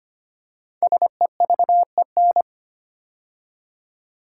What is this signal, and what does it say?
Morse code, heard as a keyed tone, about 25 wpm, 700 Hz. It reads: SEVEN
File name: 5 Letter Words - First 3 Letters Given - 25wpm